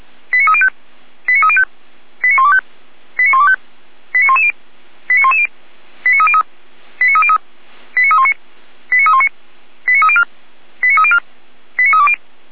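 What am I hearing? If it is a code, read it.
Five-tone selective call (ZVEI), heard as quick runs of five tones, tones of about 70 ms. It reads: 89305 89305 89125 89125 89110 89110 89303 89303 89319 89319 89304 89304 89320
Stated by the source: ZVEI five-tone selective call 89276 received by SDR, from the feuerwehr-orlamuende.org fire brigade server.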